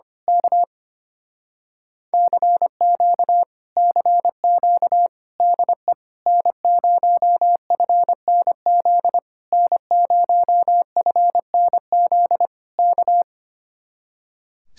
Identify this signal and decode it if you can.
Morse code, heard as a keyed tone, about 24 wpm, 700 Hz. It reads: K CQ CQ DE N0FNZ N0FNZ K
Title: Speed-Racing Common QSO Phrases - 25wpm